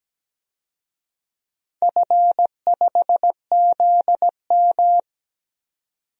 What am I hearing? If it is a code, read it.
Morse code, heard as a keyed tone, about 17 wpm, 700 Hz. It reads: F5ZM
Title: DXCC-100 Call Signs - 4 Characters - 17wpm